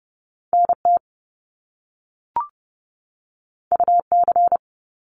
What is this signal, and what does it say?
Morse code, heard as a keyed tone, about 30 wpm, 700 Hz. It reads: NT UC
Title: Top Two Letter Combinations in Words 30wpm